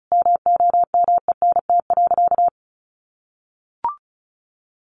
Morse code, heard as a keyed tone, about 34 wpm, 700 Hz. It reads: MOMENT.
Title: Sentences from Top 1100 Words 35wpm